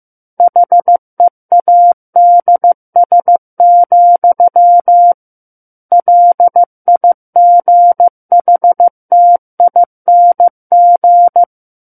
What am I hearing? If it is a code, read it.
Morse code, heard as a keyed tone, about 15 wpm, 700 Hz. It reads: HEADS, LIGHTING
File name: Chapter 6 - 15 WPM